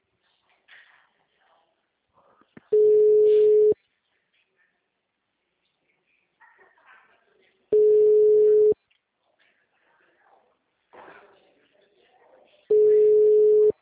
Ringback tone (Europe, 425 Hz).
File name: out- -2017- - - .79559